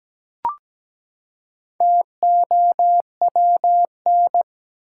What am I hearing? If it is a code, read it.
Morse code, heard as a keyed tone, about 17 wpm, 700 Hz. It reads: TOWN
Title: Four Letter Words - 17wpm